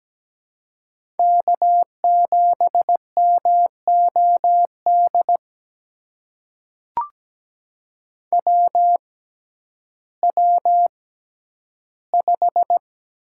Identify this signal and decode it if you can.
Morse code, heard as a keyed tone, about 17 wpm, spaced as for 10 wpm, 700 Hz. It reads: K7MOD WW5
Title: USA Call Signs - 6x Character Spacing - 17wpm